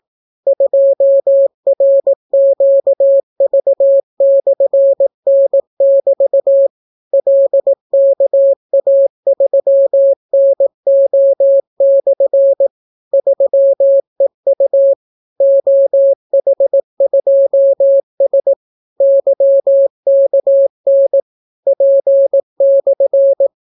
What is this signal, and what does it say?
Morse code, heard as a keyed tone, about 18 wpm, 550 Hz. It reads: 2RQV/N<BT> LKA3NO/ 3EU OH2S YKN P/